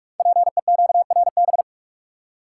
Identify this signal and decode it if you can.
Morse code, heard as a keyed tone, about 44 wpm, 700 Hz. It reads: PEQRB
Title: Code Groups - 5 Letters - 45wpm